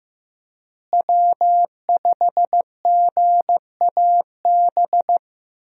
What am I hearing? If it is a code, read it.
Morse code, heard as a keyed tone, about 15 wpm, 700 Hz. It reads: W5GAB
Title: USA Call Signs - 5 Characters - Repeated 2x - 15wpm